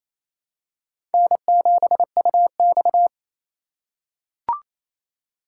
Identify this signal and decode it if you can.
Morse code, heard as a keyed tone, about 28 wpm, 700 Hz. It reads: N7UX
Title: USA Call Signs - 4x Character Spacing - 28wpm